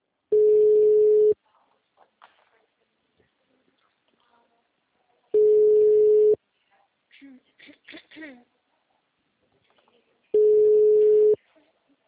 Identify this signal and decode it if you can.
Ringback tone (Europe, 425 Hz), with speech.